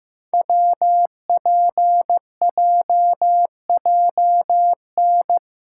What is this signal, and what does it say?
Morse code, heard as a keyed tone, about 15 wpm, 700 Hz. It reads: WPJJN
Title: Code Groups - 5 Letters - 15wpm